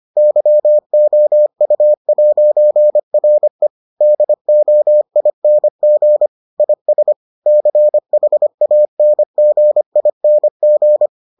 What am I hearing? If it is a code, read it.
Morse code, heard as a keyed tone, about 25 wpm, 600 Hz. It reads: YOU'RE DOING IS CHANGING